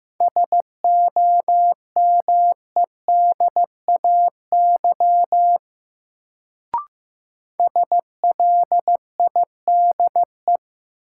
Morse code, heard as a keyed tone, about 15 wpm, 700 Hz. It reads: SOMEDAY SLIDE
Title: Sentences from Top 1600 Words - New Word Drill - 15wpm